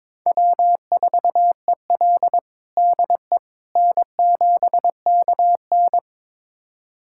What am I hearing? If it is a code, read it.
Morse code, heard as a keyed tone, about 21 wpm, 700 Hz. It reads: W4EL DE N7KN
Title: USA Callsign DE Callsign - 4 Characters - 22wpm